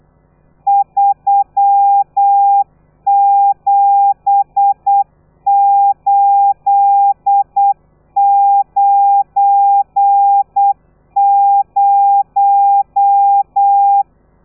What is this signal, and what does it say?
Morse code, heard as a keyed tone, about 8 wpm, 800 Hz. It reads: 37890
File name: telegrafia / i2tfj / Lez12 35